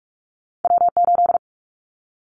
Morse code, heard as a keyed tone, about 44 wpm, 700 Hz. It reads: W8
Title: Callsign Prefixes for Top 25 DXCC - 45wpm